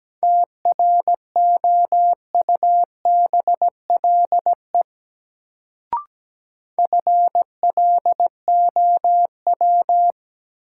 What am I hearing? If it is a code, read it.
Morse code, heard as a keyed tone, about 17 wpm, 700 Hz. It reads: TROUBLE FLOW
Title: Sentences from Top 1200 Words - New Word Drill - 17wpm